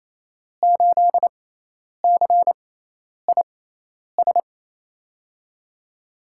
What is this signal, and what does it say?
Morse code, heard as a keyed tone, about 28 wpm, spaced as for 10 wpm, 700 Hz. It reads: 8CIS
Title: USA Call Signs - 6 Characters - 6x Character Spacing - 28wpm